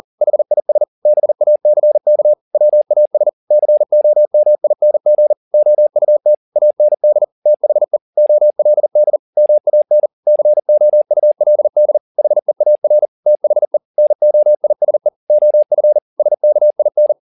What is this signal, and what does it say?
Morse code, heard as a keyed tone, about 40 wpm, 600 Hz. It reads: HIS BACK WAS COMING OUT AND THE OLD MAN COULD HEAR THE NOISE OF SKIN
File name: the old man and the sea 2 40wpm